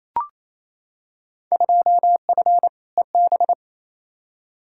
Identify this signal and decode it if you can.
Morse code, heard as a keyed tone, about 28 wpm, 700 Hz. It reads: 2F EB
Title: Field Day Exchanges Encore - 28wpm